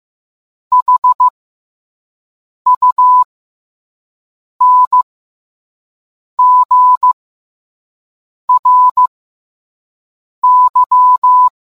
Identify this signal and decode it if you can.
Morse code, heard as a keyed tone, about 15 wpm, spaced as for 5 wpm, 1000 Hz. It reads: HUNGRY